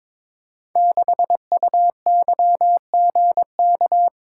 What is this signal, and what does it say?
Morse code, heard as a keyed tone, about 22 wpm, 700 Hz. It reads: 6UYGK
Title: Code Groups - 5 Letter:Numbers - 22wpm